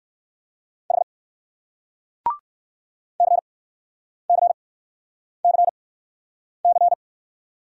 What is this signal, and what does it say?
Morse code, heard as a keyed tone, about 58 wpm, spaced as for 9 wpm, 700 Hz. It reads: H CCCC